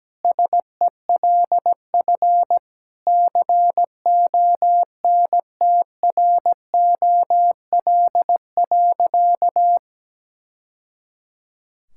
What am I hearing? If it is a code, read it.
Morse code, heard as a keyed tone, about 17 wpm, 700 Hz. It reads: SELF CONTROL.
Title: Sentences from Top 1000 Words - Review Part 5 - 17wpm